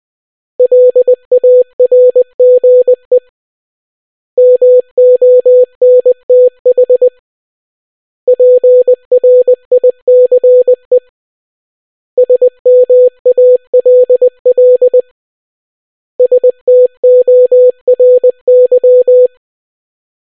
Morse code, heard as a keyed tone, about 20 wpm, 500 Hz. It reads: LARGE MONTH PRICE SMALL STORY